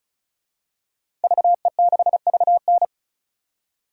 Morse code, heard as a keyed tone, about 35 wpm, 700 Hz. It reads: VE6VN
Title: DXCC-100 Call Signs - 5 Characters - 35wpm